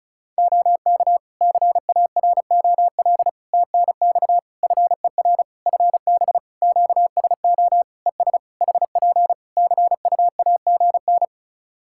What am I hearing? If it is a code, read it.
Morse code, heard as a keyed tone, about 36 wpm, 700 Hz. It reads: OK CAROL TNX FER FB QSO ES HP CUAGN
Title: Practice QSO - Exchange 4 - Ending - 35wpm